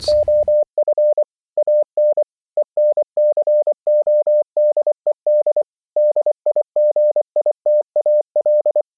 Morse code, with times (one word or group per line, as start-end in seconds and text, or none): 0.1-1.2 OF
1.6-2.2 AN
2.6-5.6 ENCODED
6.0-8.8 DIGITAL